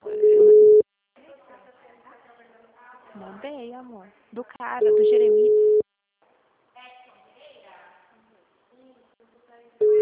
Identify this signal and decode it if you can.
Ringback tone (Europe, 425 Hz), with speech.